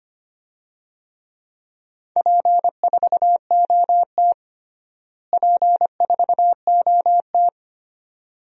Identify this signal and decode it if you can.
Morse code, heard as a keyed tone, about 25 wpm, 700 Hz. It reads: P4OT P4OT
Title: DXCC-100 Call Signs - 4 Characters - Repeated 2x - 25wpm